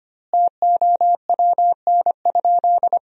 Morse code, heard as a keyed tone, about 25 wpm, 700 Hz. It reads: TOWN?
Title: 2x Word-Spacing - Sentences from Top 500 Words - 25wpm